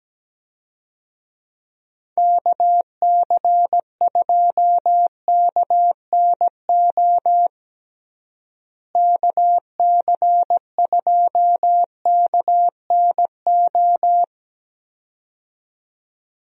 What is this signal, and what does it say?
Morse code, heard as a keyed tone, about 17 wpm, 700 Hz. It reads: KC2KNO KC2KNO
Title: USA Call Signs - 6 Characters - Repeated 2x - 17wpm